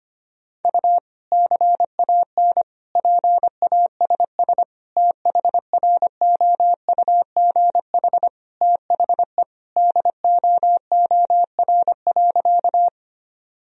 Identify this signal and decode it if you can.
Morse code, heard as a keyed tone, about 25 wpm, 700 Hz. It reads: U CAN PASS THROUGH THE DOOR.
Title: Sentences from Top 400 Words and Abbreviations 25wpm